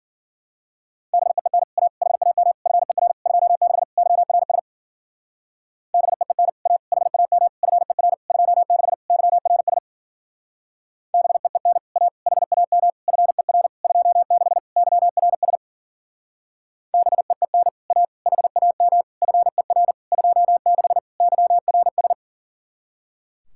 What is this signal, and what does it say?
Morse code, heard as a keyed tone, about 47 wpm, 700 Hz. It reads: BEEN A HAM FER 26 YRS BEEN A HAM FER 26 YRS BEEN A HAM FER 26 YRS BEEN A HAM FER 26 YRS
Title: Speed-Racing Common QSO Phrases - 40wpm